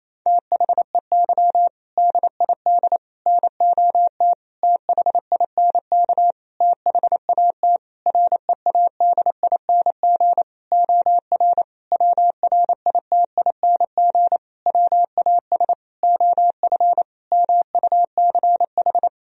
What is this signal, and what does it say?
Morse code, heard as a keyed tone, about 28 wpm, 700 Hz. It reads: THEY DID NOT THINK THAT READING OR WRITING WAS OF MUCH
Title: Sentences from Top 1000 Words - Review Part 3 - 28wpm